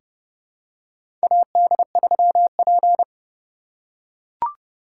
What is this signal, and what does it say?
Morse code, heard as a keyed tone, about 31 wpm, 700 Hz. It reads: AD3P
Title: USA Call Signs - 4 Characters - 3x Character Spacing - 30wpm